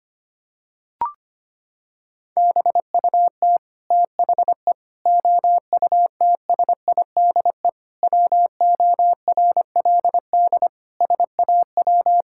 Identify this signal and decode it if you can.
Morse code, heard as a keyed tone, about 25 wpm, 700 Hz. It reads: BUT THE OUTSIDE WORLD SAW
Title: Sentences from Top 1000 Words - Review Part 6 - 25wpm